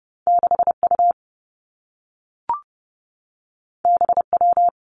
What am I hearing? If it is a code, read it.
Morse code, heard as a keyed tone, about 30 wpm, 700 Hz. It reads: 6U BW
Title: Two Challenging Letter-Number Combinations 30wpm